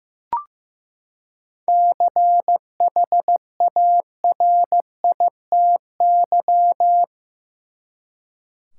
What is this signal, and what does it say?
Morse code, heard as a keyed tone, about 15 wpm, 700 Hz. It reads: CHARITY
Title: Seven Letter Words - 15wpm